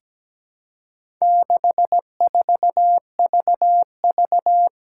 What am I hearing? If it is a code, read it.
Morse code, heard as a keyed tone, about 17 wpm, 700 Hz. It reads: 64VV
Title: Four Challenging Letter-Number Combinations 17wpm